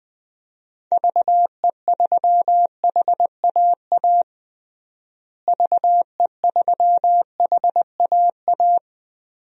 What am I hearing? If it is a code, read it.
Morse code, heard as a keyed tone, about 21 wpm, 700 Hz. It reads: VE3HAA VE3HAA
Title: DXCC-100 Call Signs - 6 Characters - Repeated 2x - 20wpm